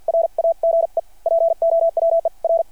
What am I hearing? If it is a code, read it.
Morse code, heard as a keyed tone, about 53 wpm, spaced as for 43 wpm, 660 Hz. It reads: RAGE POWER
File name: archived / CRSnet